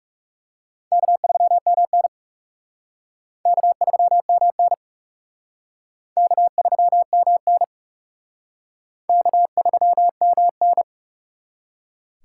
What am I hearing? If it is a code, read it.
Morse code, heard as a keyed tone, about 36 wpm, 700 Hz. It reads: K3MN K3MN K3MN K3MN
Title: Speed Racing - USA 1x2 Call Signs 30wpm